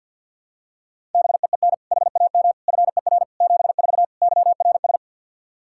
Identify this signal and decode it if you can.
Morse code, heard as a keyed tone, about 51 wpm, 700 Hz. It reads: BEEN HAM FER 74 YRS
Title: K1USN SST QSO Practice 50wpm